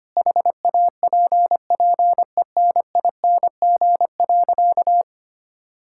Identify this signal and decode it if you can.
Morse code, heard as a keyed tone, about 25 wpm, 700 Hz. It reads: HAPPENING.
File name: practice / 25wpm / All-Age Appropriate Jokes - 25wpm